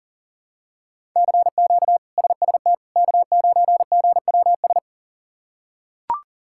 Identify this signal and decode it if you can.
Morse code, heard as a keyed tone, about 40 wpm, 700 Hz. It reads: CQ SST K9GWS
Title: K1USN SST QSO Practice - 40wpm